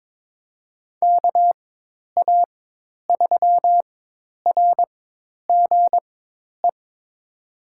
Morse code, heard as a keyed tone, about 22 wpm, spaced as for 10 wpm, 700 Hz. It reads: KA3RGE
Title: USA Call Signs - 4x Character Spacing - 22wpm